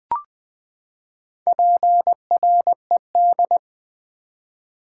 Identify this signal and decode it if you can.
Morse code, heard as a keyed tone, about 20 wpm, 700 Hz. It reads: PRED
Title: Build A Word - 4 Letters At A Time - 11-Letter Words - 20wpm